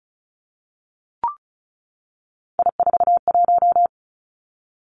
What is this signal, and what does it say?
Morse code, heard as a keyed tone, about 36 wpm, 700 Hz. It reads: I41
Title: Three Challenging Letter-Number Combinations 35wpm